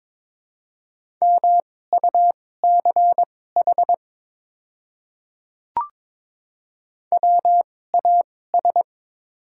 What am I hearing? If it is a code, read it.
Morse code, heard as a keyed tone, about 22 wpm, spaced as for 16 wpm, 700 Hz. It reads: MUCH WAS
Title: 2x Character Spacing - Top 500 Words - 22wpm